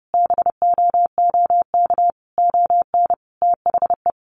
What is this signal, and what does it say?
Morse code, heard as a keyed tone, about 30 wpm, 700 Hz. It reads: BOOK ON THE
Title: Sentences from Top 1400 Words 30wpm